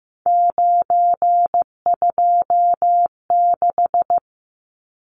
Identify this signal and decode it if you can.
Morse code, heard as a keyed tone, about 15 wpm, 700 Hz. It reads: 926
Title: Numbers - 3-Digit 15wpm